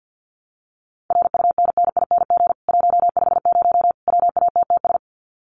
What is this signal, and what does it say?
Morse code, heard as a keyed tone, about 50 wpm, 700 Hz. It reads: RUNNING 150 WATTS
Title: Common QSO Phrases 50wpm